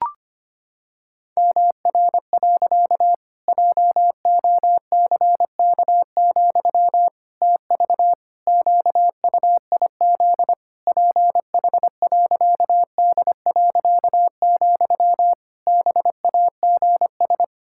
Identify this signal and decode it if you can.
Morse code, heard as a keyed tone, about 25 wpm, 700 Hz. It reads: MR. JOCK, TV QUIZ PH.D., BAGS
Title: Pangram Challenge v2 - 25wpm